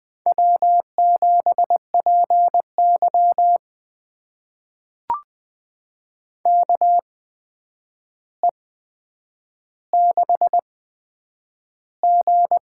Morse code, heard as a keyed tone, about 20 wpm, spaced as for 9 wpm, 700 Hz. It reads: W7PY KE6G
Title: USA Call Signs - 8x Character Spacing - 20wpm